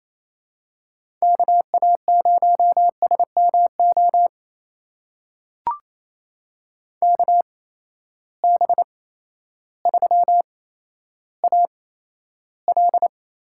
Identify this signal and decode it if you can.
Morse code, heard as a keyed tone, about 28 wpm, spaced as for 13 wpm, 700 Hz. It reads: KA0SMO KB3AL
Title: USA Call Signs - 6 Characters - 8x Character Spacing - 28wpm